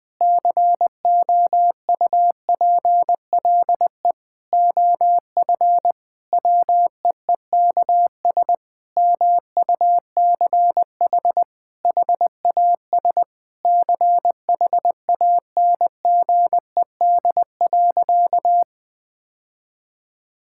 Morse code, heard as a keyed tone, about 20 wpm, 700 Hz. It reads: COUPLE OF WEEKS MUCH HAS CHANGED.
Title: Sentences with 5-Word Phrases - Spoken Beginning - 20wpm